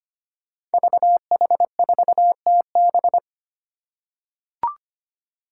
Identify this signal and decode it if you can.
Morse code, heard as a keyed tone, about 25 wpm, 700 Hz. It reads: VH4TB
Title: Five Challenging Letter-Number Combinations 25wpm